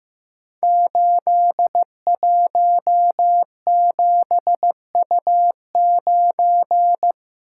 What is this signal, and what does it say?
Morse code, heard as a keyed tone, about 15 wpm, 700 Hz. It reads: 817U9
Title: Code Groups - 5 Letter:Numbers - 15wpm